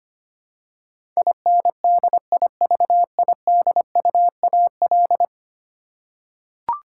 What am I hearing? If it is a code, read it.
Morse code, heard as a keyed tone, about 25 wpm, 700 Hz. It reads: INDIVIDUAL
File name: Ten Letter Words - 25wpm